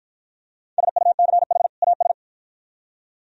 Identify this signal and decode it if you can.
Morse code, heard as a keyed tone, about 48 wpm, 700 Hz. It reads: SUCH AS